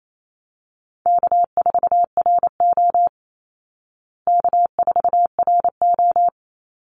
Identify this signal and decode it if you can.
Morse code, heard as a keyed tone, about 28 wpm, 700 Hz. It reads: K4RO K4RO
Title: Easy Call Signs Repeated 3x - 28wpm